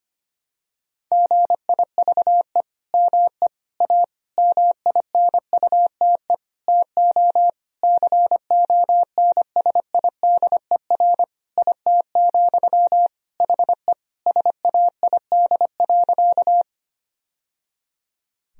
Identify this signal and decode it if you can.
Morse code, heard as a keyed tone, about 25 wpm, 700 Hz. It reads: GIVE ME A MINUTE TO CONSIDER IT, HE SAID.
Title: Sentences from Top 1000 Words - Review Part 4 - 25wpm